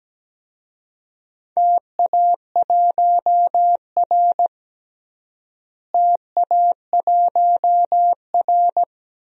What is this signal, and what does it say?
Morse code, heard as a keyed tone, about 17 wpm, 700 Hz. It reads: TA1R TA1R
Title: DXCC-100 Call Signs - 4 Characters - Repeated 2x - 17wpm